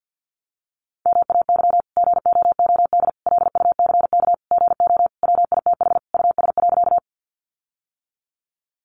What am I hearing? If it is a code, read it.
Morse code, heard as a keyed tone, about 50 wpm, 700 Hz. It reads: MAY GOOD LUCK GO WITH US.
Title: Sentences from Top 1500 Words 50wpm